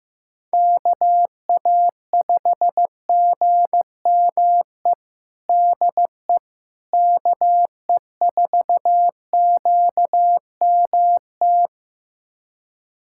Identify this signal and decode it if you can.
Morse code, heard as a keyed tone, about 15 wpm, 700 Hz. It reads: KA5GME DE KE4QMT
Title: USA Callsign DE Callsign - 6 Characters - 15wpm